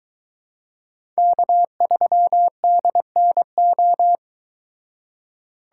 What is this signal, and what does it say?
Morse code, heard as a keyed tone, about 22 wpm, 700 Hz. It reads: K3DNO